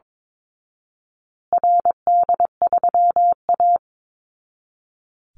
Morse code, heard as a keyed tone, about 22 wpm, 700 Hz. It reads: RD3A
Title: Call Signs Repeated 2x - 22wpm